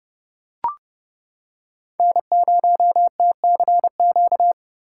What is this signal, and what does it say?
Morse code, heard as a keyed tone, about 29 wpm, 700 Hz. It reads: N0TCQ